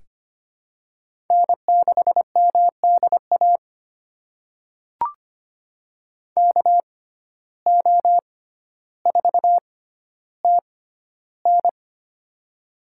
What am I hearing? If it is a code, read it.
Morse code, heard as a keyed tone, about 25 wpm, spaced as for 13 wpm, 700 Hz. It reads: N6MDA KO4TN
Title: USA Call Signs - 6x Character Spacing - 25wpm